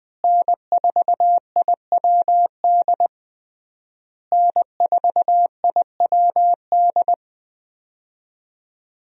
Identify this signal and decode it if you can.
Morse code, heard as a keyed tone, about 20 wpm, 700 Hz. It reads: N4IWD N4IWD
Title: USA Call Signs - 5 Characters - Repeated 3x - 20wpm